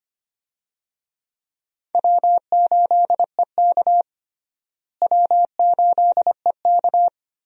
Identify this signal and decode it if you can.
Morse code, heard as a keyed tone, about 25 wpm, 700 Hz. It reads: W8EK W8EK
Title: USA Call Signs - 4 Characters - Repeated 2x - 25wpm